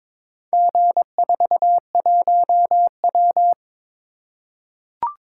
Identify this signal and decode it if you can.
Morse code, heard as a keyed tone, about 22 wpm, 700 Hz. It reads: G41W
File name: practice / 22wpm / Four Letter-Number Combinations - 22wpm